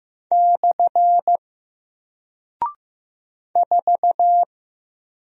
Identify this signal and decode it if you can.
Morse code, heard as a keyed tone, about 15 wpm, 700 Hz. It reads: / 4
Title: Single Character (Letter:Number:Punctuation:Prosign) Alt 15wpm